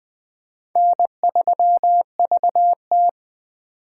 Morse code, heard as a keyed tone, about 20 wpm, 700 Hz. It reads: N3VT
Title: Speed Racing - USA 1x2 Call Signs 17wpm